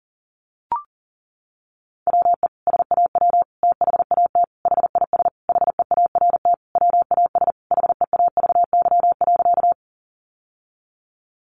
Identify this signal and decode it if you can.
Morse code, heard as a keyed tone, about 40 wpm, 700 Hz. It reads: WE SAW THAT HIS HEART WAS HEAVY.
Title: Sentences from Top 1200 Words 40wpm